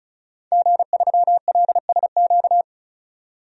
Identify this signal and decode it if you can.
Morse code, heard as a keyed tone, about 36 wpm, 700 Hz. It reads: G3LSQ
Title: DXCC-100 Call Signs - 5 Characters - 35wpm